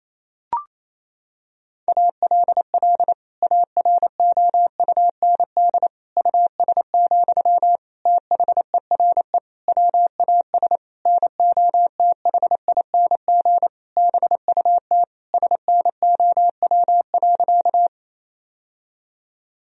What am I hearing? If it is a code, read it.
Morse code, heard as a keyed tone, about 28 wpm, 700 Hz. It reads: ALL AROUND US, THERE WAS NOTHING BUT SNOW.
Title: Sentences from Top 1000 Words - Review Part 3 - 28wpm